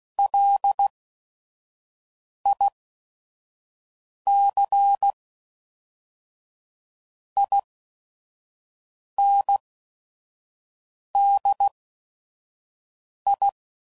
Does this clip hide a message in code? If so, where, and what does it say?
Yes: Morse code from 0.2 to 13.5 s, heard as a keyed tone, about 16 wpm, 800 Hz. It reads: L I C I N D I